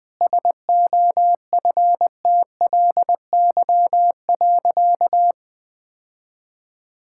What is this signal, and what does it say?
Morse code, heard as a keyed tone, about 20 wpm, 700 Hz. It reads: SOFTLY.
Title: Easy Sentences with Hints - 20wpm